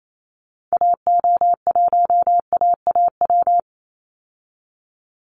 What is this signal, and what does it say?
Morse code, heard as a keyed tone, about 28 wpm, 700 Hz. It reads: AO1AAW
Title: Call Signs Repeated 3x - 28wpm